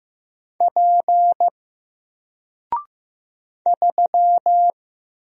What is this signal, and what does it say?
Morse code, heard as a keyed tone, about 15 wpm, 700 Hz. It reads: P 3
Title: Single Letter-Number Alt 15wpm